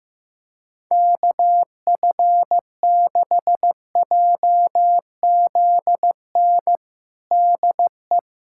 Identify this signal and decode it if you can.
Morse code, heard as a keyed tone, about 15 wpm, 700 Hz. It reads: KF6JZN DE
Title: USA Callsign DE Callsign - 6 Characters - 15wpm